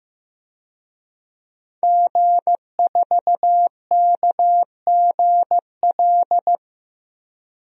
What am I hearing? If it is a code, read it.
Morse code, heard as a keyed tone, about 15 wpm, 700 Hz. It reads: G4KGL
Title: DXCC-100 Call Signs - 5 Characters - 15wpm